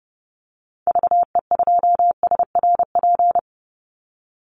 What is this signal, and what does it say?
Morse code, heard as a keyed tone, about 30 wpm, 700 Hz. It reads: VE2SRP
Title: Call Signs Repeated 3x - 30wpm